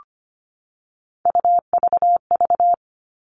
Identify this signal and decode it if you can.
Morse code, heard as a keyed tone, about 25 wpm, 700 Hz. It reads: UVV
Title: All Cut Numbers - 3-Digit 25wpm